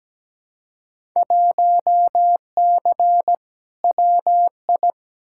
Morse code, heard as a keyed tone, about 17 wpm, 700 Hz. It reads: 1C WI
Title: Field Day Exchanges - 17wpm